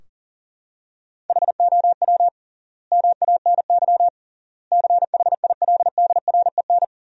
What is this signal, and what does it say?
Morse code, heard as a keyed tone, about 40 wpm, 700 Hz. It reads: HOW MANY CHILDREN
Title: 3x Word-Spacing - Sentences from Top 400 Words - 40wpm